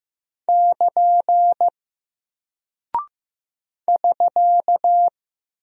Morse code, heard as a keyed tone, about 15 wpm, 700 Hz. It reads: <KN> <SK>
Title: Single Character (Letter:Number:Punctuation:Prosign) Alt 15wpm